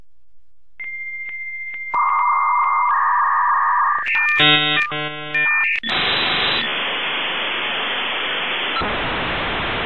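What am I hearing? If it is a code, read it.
Dial-up modem handshake.
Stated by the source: Dial-up 56k modem connection sound.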